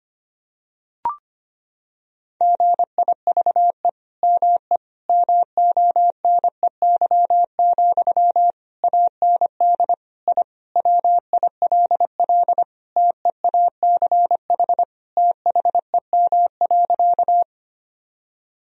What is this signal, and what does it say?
Morse code, heard as a keyed tone, about 25 wpm, 700 Hz. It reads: GIVE ME MONEY, AND I WILL TEACH THEM.
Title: Sentences from Top 1000 Words - Review Part 2 - 25wpm